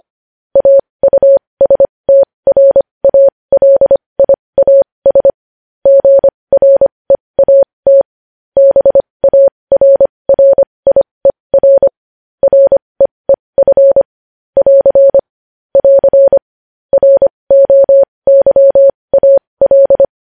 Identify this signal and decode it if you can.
Morse code, heard as a keyed tone, about 25 wpm, spaced as for 20 wpm, 550 Hz. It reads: AUSTRALIAS GREAT BARRIER REEF <AR> <AR> ROYAL
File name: podcast / News headlines-1x-25x20